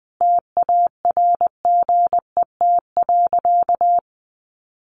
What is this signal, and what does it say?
Morse code, heard as a keyed tone, about 20 wpm, 700 Hz. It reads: TARGET.
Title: Sentences from Top 1200 Words 20wpm